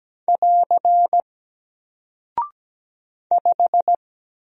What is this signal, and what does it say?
Morse code, heard as a keyed tone, about 17 wpm, 700 Hz. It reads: <AR> 5
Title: Single Character (Letter:Number:Punctuation:Prosign) Alt 17wpm